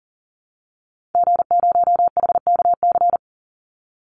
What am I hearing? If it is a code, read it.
Morse code, heard as a keyed tone, about 40 wpm, 700 Hz. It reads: G0HKC
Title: Call Signs Repeated 2x - 40wpm